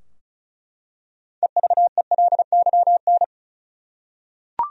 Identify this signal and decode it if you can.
Morse code, heard as a keyed tone, about 36 wpm, 700 Hz. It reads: EVELYN
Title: USA Common Names - 35wpm